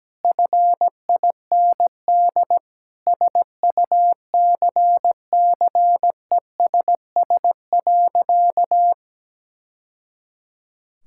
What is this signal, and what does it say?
Morse code, heard as a keyed tone, about 17 wpm, 700 Hz. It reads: FIND SUCCESS.
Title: Sentences from Top 1000 Words - Review Part 5 - 17wpm